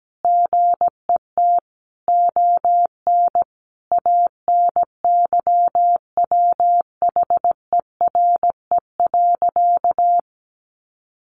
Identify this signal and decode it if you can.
Morse code, heard as a keyed tone, about 17 wpm, 700 Hz. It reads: GET ON ANYWHERE.
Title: Sentences from Top 1400 Words 17wpm